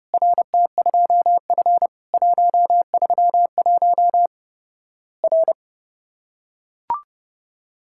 Morse code, heard as a keyed tone, about 30 wpm, 700 Hz. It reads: RT2F 131 R
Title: Worked All Europe DX Contest - QTC Exchange - 30wpm